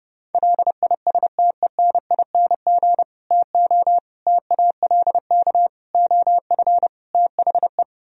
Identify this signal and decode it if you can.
Morse code, heard as a keyed tone, about 33 wpm, 700 Hz. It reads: LISTENING TO TALK OF THE